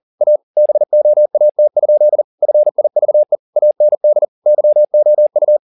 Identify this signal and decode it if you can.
Morse code, heard as a keyed tone, about 40 wpm, 600 Hz. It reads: A BOAT? FIVE AND YOU